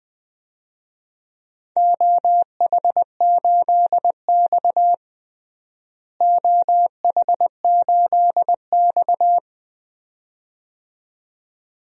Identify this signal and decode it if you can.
Morse code, heard as a keyed tone, about 20 wpm, 700 Hz. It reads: OH8X OH8X
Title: DXCC-100 Call Signs - 4 Characters - Repeated 2x - 20wpm